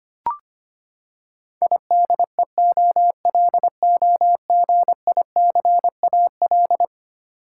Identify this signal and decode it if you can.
Morse code, heard as a keyed tone, about 25 wpm, 700 Hz. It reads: IDEOLOGICAL
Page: morse-code-ninja-practice-25wpm